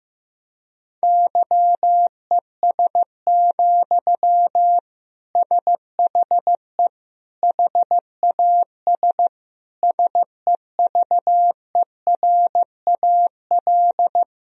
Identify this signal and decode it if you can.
Morse code, heard as a keyed tone, about 16 wpm, 700 Hz. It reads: YES, SHE HAS SEVERAL
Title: Sentences from Top 1000 Words - Review Part 2 - 15wpm